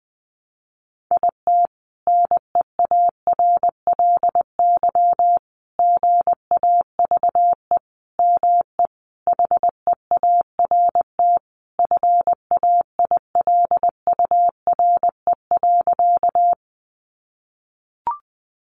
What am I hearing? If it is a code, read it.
Morse code, heard as a keyed tone, about 20 wpm, 700 Hz. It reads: IT NEARLY GAVE ME HEART FAILURE.
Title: Sentences from Top 1300 Words 20wpm